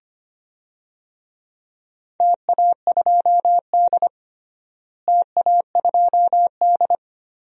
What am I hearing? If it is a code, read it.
Morse code, heard as a keyed tone, about 25 wpm, 700 Hz. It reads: TA2D TA2D
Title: DXCC-100 Call Signs - 4 Characters - Repeated 2x - 25wpm